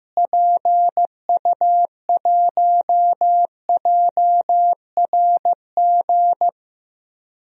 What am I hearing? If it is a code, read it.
Morse code, heard as a keyed tone, about 15 wpm, 700 Hz. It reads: PU1JRG
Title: DXCC-100 Call Signs - 6 Characters - Repeated 2x - 15wpm